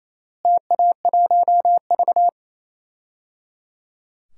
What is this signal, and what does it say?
Morse code, heard as a keyed tone, about 28 wpm, 700 Hz. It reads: TA1V
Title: DXCC-100 Call Signs - 4 Characters - Repeated 3x - 28wpm